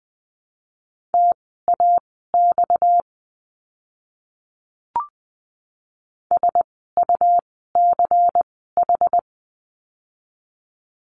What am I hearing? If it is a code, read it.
Morse code, heard as a keyed tone, about 20 wpm, spaced as for 15 wpm, 700 Hz. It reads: TAX SUCH
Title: 230 Lesson 37f - Letter x - Words 20wpm